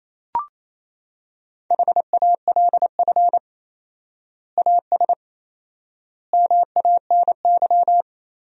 Morse code, heard as a keyed tone, about 28 wpm, 700 Hz. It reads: HALF AS MANY
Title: 4x Word-Spacing - Sentences from Top 400 Words - 28wpm